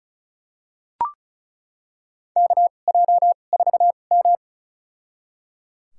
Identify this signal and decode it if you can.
Morse code, heard as a keyed tone, about 36 wpm, spaced as for 25 wpm, 700 Hz. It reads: KJ4M
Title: USA Call Signs - 4 Characters - 2x Character Spacing - 35wpm